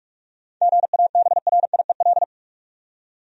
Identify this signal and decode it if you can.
Morse code, heard as a keyed tone, about 46 wpm, 700 Hz. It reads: GABRIEL
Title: USA Common Names - 45wpm